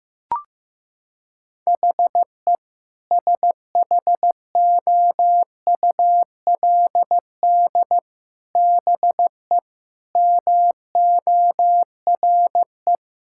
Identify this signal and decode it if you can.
Morse code, heard as a keyed tone, about 16 wpm, 700 Hz. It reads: HE SHOULD BE MORE
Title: Sentences from Top 1600 Words 15wpm